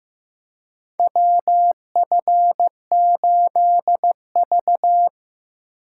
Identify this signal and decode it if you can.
Morse code, heard as a keyed tone, about 15 wpm, 700 Hz. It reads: WF8V
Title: USA Call Signs - 4 Characters - 4x Character Spacing - 15wpm